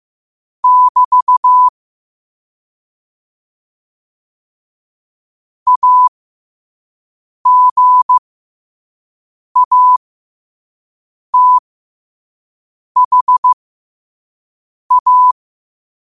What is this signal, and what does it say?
Morse code, heard as a keyed tone, about 15 wpm, spaced as for 5 wpm, 1000 Hz. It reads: <BT> AGATHA
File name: qotdfiles / 5wpm